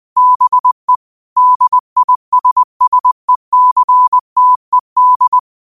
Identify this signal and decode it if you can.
Morse code, heard as a keyed tone, about 20 wpm, 1000 Hz. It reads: BE DISSECTED
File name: qotdfiles / 20wpm